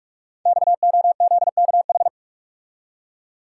Morse code, heard as a keyed tone, about 44 wpm, 700 Hz. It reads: XOZKH
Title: Code Groups - 5 Letter:Numbers - 45wpm